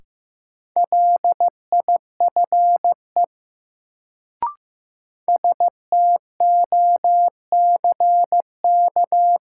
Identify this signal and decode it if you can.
Morse code, heard as a keyed tone, about 15 wpm, 700 Hz. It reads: LIFE STOCK
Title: Sets of 5 Words - New Word Drill - 15wpm